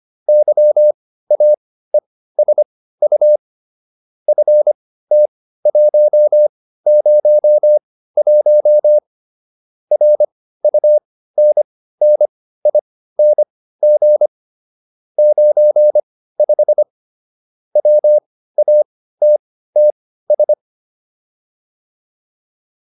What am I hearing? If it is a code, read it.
Morse code, heard as a keyed tone, about 25 wpm, spaced as for 15 wpm, 600 Hz. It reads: YAESU FT101 RUNNING 95 WATTS
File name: QSO409 15